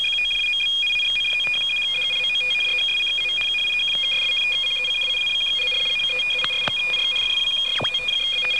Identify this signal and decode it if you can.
Two-tone teleprinter signal, about 850 Hz shift.
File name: DZF 12828 1612